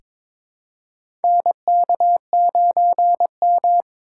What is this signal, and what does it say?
Morse code, heard as a keyed tone, about 22 wpm, 700 Hz. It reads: NK9M
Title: USA Call Signs - 4 Characters - 4x Character Spacing - 22wpm